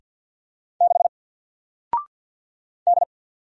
Morse code, heard as a keyed tone, about 50 wpm, 700 Hz. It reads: 6 D
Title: Single Character (Letter:Number:Punctuation:Prosign) 50wpm